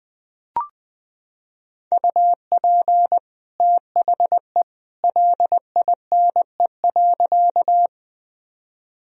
Morse code, heard as a keyed tone, about 20 wpm, 700 Hz. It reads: UP THE LINE.
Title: Sentences from Top 1000 Words - Review Part 1 - 20wpm